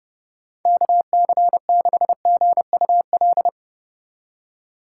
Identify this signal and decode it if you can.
Morse code, heard as a keyed tone, about 38 wpm, 700 Hz. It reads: KC6GUL